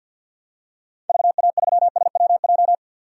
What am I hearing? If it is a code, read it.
Morse code, heard as a keyed tone, about 51 wpm, 700 Hz. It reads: VA3SWJ
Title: DXCC-100 Call Signs - 6 Characters - 50wpm